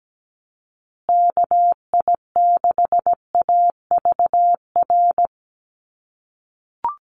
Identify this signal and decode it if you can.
Morse code, heard as a keyed tone, about 17 wpm, 700 Hz. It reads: KI6AVR
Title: USA Call Signs Repeated 2x 17wpm